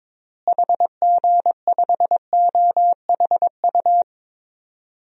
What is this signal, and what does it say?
Morse code, heard as a keyed tone, about 22 wpm, 700 Hz. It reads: HG5OHU
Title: DXCC-100 Call Signs - 6 Characters - Repeated 3x - 22wpm